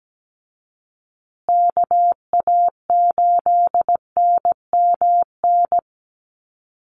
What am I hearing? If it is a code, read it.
Morse code, heard as a keyed tone, about 17 wpm, 700 Hz. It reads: KA8NMN
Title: USA Call Signs Repeated 3x 17wpm